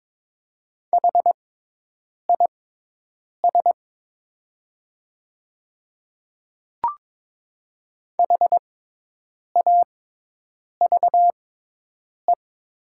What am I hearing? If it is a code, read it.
Morse code, heard as a keyed tone, about 22 wpm, spaced as for 8 wpm, 700 Hz. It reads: HIS HAVE
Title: 6x Character Spacing - Top 100 Words - 22wpm